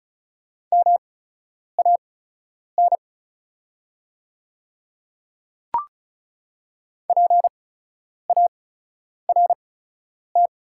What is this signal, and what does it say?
Morse code, heard as a keyed tone, about 36 wpm, spaced as for 10 wpm, 690 Hz. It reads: MAN PART
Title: 8x Character Spacing - Top 200 Words - 35wpm